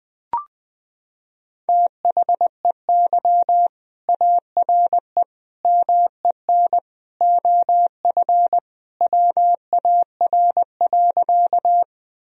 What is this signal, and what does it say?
Morse code, heard as a keyed tone, about 20 wpm, 700 Hz. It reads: THEY ARE MEN OF WAR.
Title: Sentences from Top 1000 Words - Review Part 2 - 20wpm